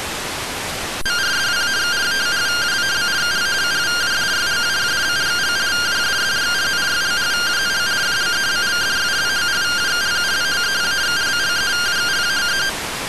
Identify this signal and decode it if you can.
Two-tone teleprinter signal, about 170 Hz shift.
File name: findings / MR AXIAL 33,99mm 1.gen-06 rtty rx